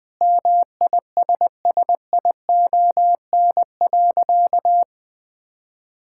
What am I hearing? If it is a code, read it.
Morse code, heard as a keyed tone, about 21 wpm, 700 Hz. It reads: MISSION.
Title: Sentences from Top 1600 Words 20wpm